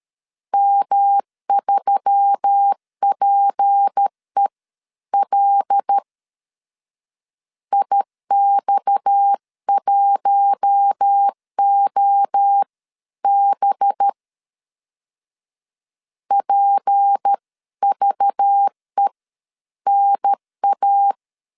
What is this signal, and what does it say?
Morse code, heard as a keyed tone, about 13 wpm, spaced as for 11 wpm, 800 Hz. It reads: M3PEL IX1OB PVENA
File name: CW / DET / 1 6 1500C S B 12wpm 3431